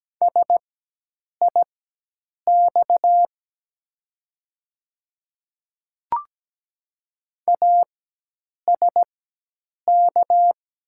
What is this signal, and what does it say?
Morse code, heard as a keyed tone, about 17 wpm, spaced as for 8 wpm, 700 Hz. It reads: SIX ASK
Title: 4x Character Spacing - Top 500 Words - 17wpm